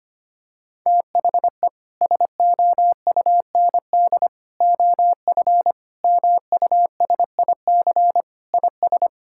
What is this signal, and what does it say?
Morse code, heard as a keyed tone, about 25 wpm, 700 Hz. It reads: THE SOUND OF MUSIC IS
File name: practice / 25wpm / Sentences with 3-Word Phrases - 25wpm